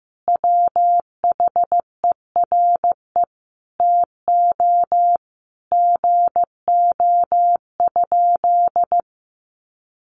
Morse code, heard as a keyed tone, about 15 wpm, 700 Hz. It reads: WHERE TO GO?
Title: Sentences from Top 1300 Words 15wpm